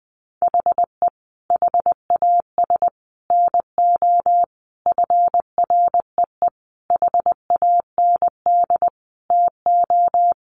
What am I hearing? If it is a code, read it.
Morse code, heard as a keyed tone, about 20 wpm, 700 Hz. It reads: HE HAS NO FREE HAND TO
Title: Sentences from Top 1400 Words 20wpm